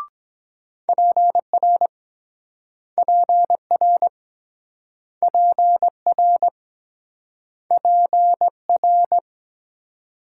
Morse code, heard as a keyed tone, about 22 wpm, 700 Hz. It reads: PR PR PR PR